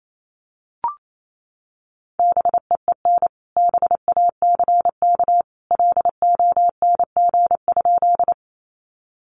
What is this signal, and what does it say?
Morse code, heard as a keyed tone, about 28 wpm, 700 Hz. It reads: BEEN BACK LONG?
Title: Sentences from Top 200 Words 28wpm